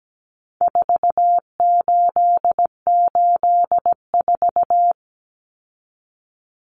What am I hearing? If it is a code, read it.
Morse code, heard as a keyed tone, about 17 wpm, 700 Hz. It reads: 4884
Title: Common Cut Numbers - 4-Digit 17wpm